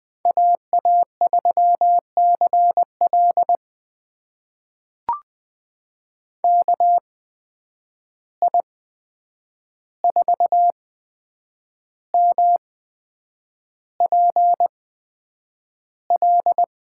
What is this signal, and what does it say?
Morse code, heard as a keyed tone, about 20 wpm, spaced as for 8 wpm, 700 Hz. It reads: AA3CL KI4MPL
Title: USA Call Signs - 8x Character Spacing - 20wpm